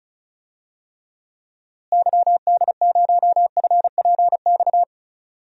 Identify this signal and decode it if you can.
Morse code, heard as a keyed tone, about 36 wpm, 700 Hz. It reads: YD0FPX
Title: DXCC-100 Call Signs - 6 Characters - Repeated 3x - 35wpm